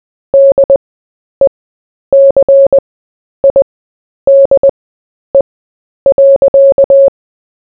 Morse code, heard as a keyed tone, about 21 wpm, spaced as for 10 wpm, 550 Hz. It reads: DECIDE.
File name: cw / fists / Short-Story-4-10wpm